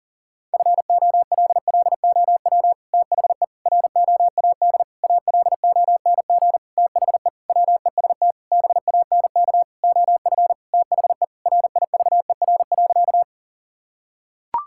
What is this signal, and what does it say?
Morse code, heard as a keyed tone, about 40 wpm, 700 Hz. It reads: FOLLOW THE ROAD ALONG THE WEST BANK OF THE RIVER.
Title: Sentences from Top 1000 Words - Review Part 6 - 40wpm